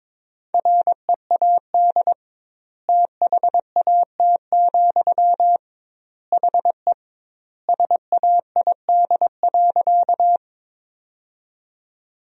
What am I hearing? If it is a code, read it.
Morse code, heard as a keyed tone, about 22 wpm, 700 Hz. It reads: READ THAT, HE SAID.
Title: 2x Word-Spacing - Sentences from Top 200 Words - 22wpm